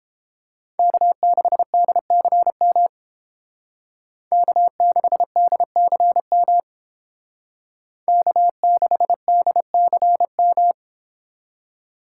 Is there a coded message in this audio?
Yes: Morse code from 0.8 to 10.7 s, heard as a keyed tone, about 28 wpm, 700 Hz. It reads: K6DCM K6DCM K6DCM